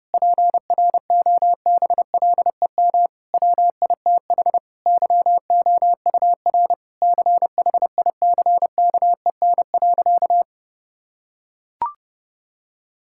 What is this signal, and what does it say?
Morse code, heard as a keyed tone, about 30 wpm, 700 Hz. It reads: PROBLEM WITH YOUR CHICKEN.
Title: Sentences from Top 1000 Words - Review Part 5 - 30wpm